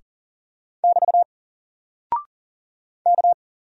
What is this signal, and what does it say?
Morse code, heard as a keyed tone, about 40 wpm, 700 Hz. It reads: <BT> K
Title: Single Character (Letter:Number:Punctuation:Prosign) Alt 40wpm